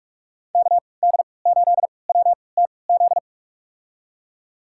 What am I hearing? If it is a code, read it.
Morse code, heard as a keyed tone, about 46 wpm, spaced as for 25 wpm, 690 Hz. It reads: KD8WTZ
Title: USA Call Signs - 6 Characters - 3x Character Spacing - 45wpm